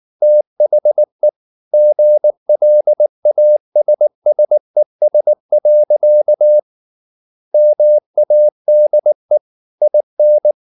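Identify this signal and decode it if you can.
Morse code, heard as a keyed tone, about 19 wpm, 600 Hz. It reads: THE GLASSES. MADE IN
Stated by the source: Morse code at 19 wpm from CW med Gnister, reading Ken Follett's novel Never in English.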